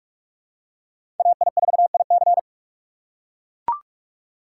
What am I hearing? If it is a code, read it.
Morse code, heard as a keyed tone, about 44 wpm, 690 Hz. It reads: AI4IC
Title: USA Call Signs - 5 Characters - 6x Character Spacing - 45wpm